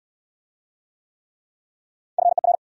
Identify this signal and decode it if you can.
Morse code, heard as a keyed tone, about 69 wpm, 700 Hz. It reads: FER